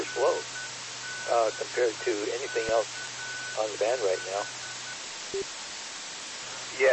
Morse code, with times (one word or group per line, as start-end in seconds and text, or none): none